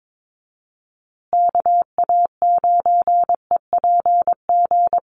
Morse code, heard as a keyed tone, about 22 wpm, 700 Hz. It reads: KA9EPG
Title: USA Call Signs Repeated 3x 22wpm